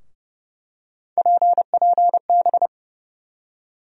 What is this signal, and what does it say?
Morse code, heard as a keyed tone, about 30 wpm, 700 Hz. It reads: PPB
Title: USA Call Sign Postfixes - 3 Letters 30wpm